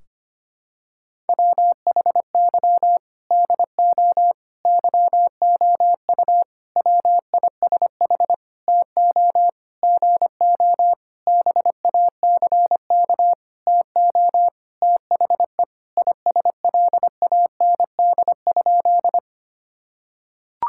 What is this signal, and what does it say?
Morse code, heard as a keyed tone, about 25 wpm, 700 Hz. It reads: WHY DO YOU WISH TO GO BACK TO THE ISLAND?
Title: Sentences from Top 1600 Words 25wpm